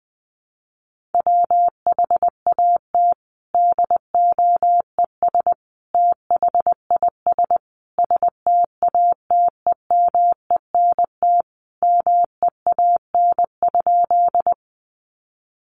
Morse code, heard as a keyed tone, about 20 wpm, 700 Hz. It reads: WHAT DOES THIS STATEMENT MEAN?
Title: Sentences from Top 1200 Words 20wpm